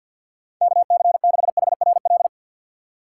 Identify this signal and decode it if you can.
Morse code, heard as a keyed tone, about 52 wpm, 690 Hz. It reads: KC6HRL
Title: USA Call Signs - 6 Characters - 3x Character Spacing - 50wpm